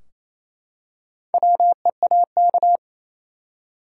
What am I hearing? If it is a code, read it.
Morse code, heard as a keyed tone, about 28 wpm, 700 Hz. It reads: WEAK
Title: Four Letter Words - 28wpm